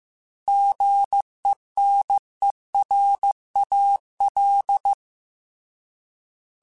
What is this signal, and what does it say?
Morse code, heard as a keyed tone, about 15 wpm, 800 Hz. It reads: GENERAL